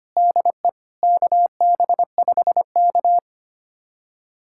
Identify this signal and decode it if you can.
Morse code, heard as a keyed tone, about 25 wpm, 700 Hz. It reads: DE KB5K
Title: USA Callsign DE Callsign - 4 Characters - 25wpm